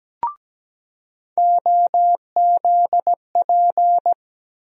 Morse code, heard as a keyed tone, about 17 wpm, 700 Hz. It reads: OZP
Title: USA Call Sign Postfixes - 3 Letters 17wpm